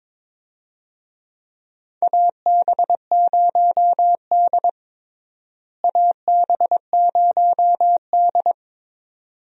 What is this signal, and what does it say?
Morse code, heard as a keyed tone, about 22 wpm, 700 Hz. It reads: AB0D AB0D
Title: USA Call Signs - 4 Characters - Repeated 2x - 22wpm